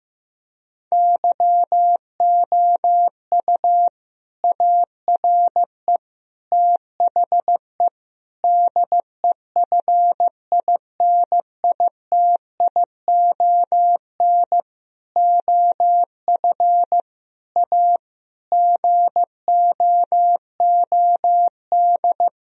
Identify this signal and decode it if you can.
Morse code, heard as a keyed tone, about 15 wpm, 700 Hz. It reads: YOU ARE THE DEFINITION OF A GOOD
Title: Sentences from Top 1000 Words - Review Part 4 - 15wpm